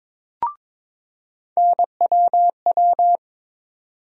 Morse code, heard as a keyed tone, about 21 wpm, 700 Hz. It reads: NWW
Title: USA Call Sign Postfixes - 3 Letters 22wpm